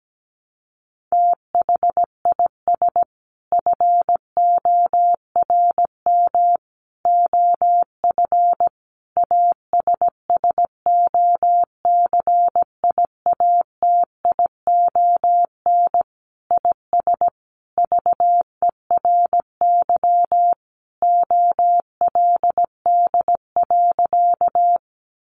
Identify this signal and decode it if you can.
Morse code, heard as a keyed tone, about 17 wpm, 700 Hz. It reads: THIS FORM OF ASSOCIATION IS VERY OLD.
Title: Sentences from Top 1100 Words 17wpm